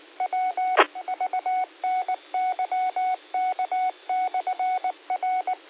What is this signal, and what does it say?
Morse code, heard as a keyed tone, about 19 wpm, 740 Hz. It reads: W VNYK/R
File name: qsorec SimplexLogic